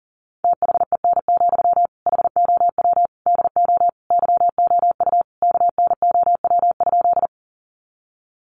Morse code, heard as a keyed tone, about 37 wpm, 700 Hz. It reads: THEN, HOW DO YOU KNOW?
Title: Sentences from Top 100 Words 40wpm